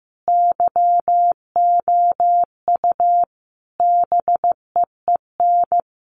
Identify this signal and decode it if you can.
Morse code, heard as a keyed tone, about 15 wpm, 700 Hz. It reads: YOU BEEN
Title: Sentences from Top 200 Words 15wpm